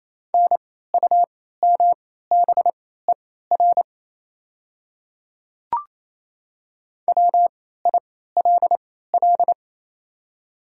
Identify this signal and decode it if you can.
Morse code, heard as a keyed tone, about 28 wpm, spaced as for 16 wpm, 700 Hz. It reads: NUMBER WILL
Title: 3x Character Spacing - Top 100 Words - 28wpm